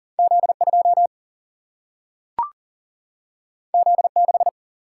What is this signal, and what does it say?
Morse code, heard as a keyed tone, about 40 wpm, 700 Hz. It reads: Z2 Z6
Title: Single-Letter Single-Number 40wpm